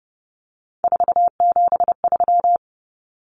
Morse code, heard as a keyed tone, about 29 wpm, 700 Hz. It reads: 473
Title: Numbers - 3-Digit 30wpm